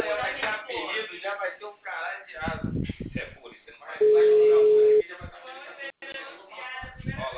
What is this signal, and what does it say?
Ringback tone (Europe, 425 Hz), with speech.
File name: out- -2012- - - .47135